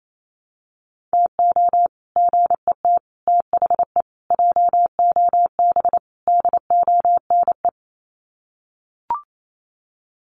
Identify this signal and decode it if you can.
Morse code, heard as a keyed tone, about 28 wpm, 700 Hz. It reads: TO GET THE JOB DONE
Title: Sets of 5 Words Encore2 28wpm